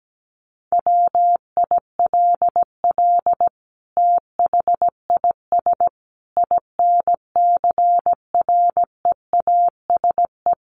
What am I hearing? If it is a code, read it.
Morse code, heard as a keyed tone, about 17 wpm, 700 Hz. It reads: WILL THIS INCREASE
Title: Sentences from Top 1200 Words 17wpm